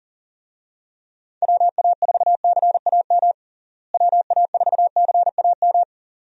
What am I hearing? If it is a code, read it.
Morse code, heard as a keyed tone, about 40 wpm, 700 Hz. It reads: WA4CAM WA4CAM
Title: USA Call Signs - 6 Characters - Repeated 2x - 40wpm